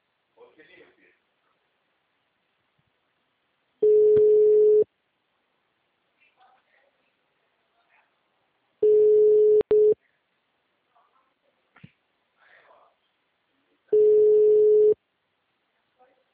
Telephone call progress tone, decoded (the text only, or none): ringback tone